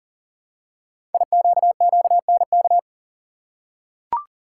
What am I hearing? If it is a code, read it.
Morse code, heard as a keyed tone, about 39 wpm, 700 Hz. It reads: IQQNK
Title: Code Groups - 5 Letters - 40wpm